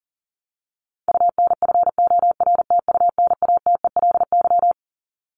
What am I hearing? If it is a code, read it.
Morse code, heard as a keyed tone, about 40 wpm, 700 Hz. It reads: UNFORTUNATELY
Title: Top 1751-2000 Words 40wpm